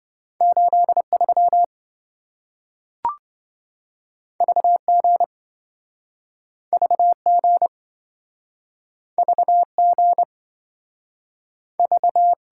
Morse code, heard as a keyed tone, about 27 wpm, 700 Hz. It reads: 83 VG VG VG V